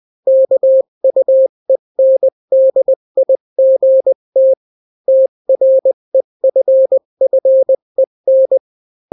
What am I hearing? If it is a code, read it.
Morse code, heard as a keyed tone, about 20 wpm, 530 Hz. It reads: KUENDIGT TREFFEN
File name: daily / 18wpm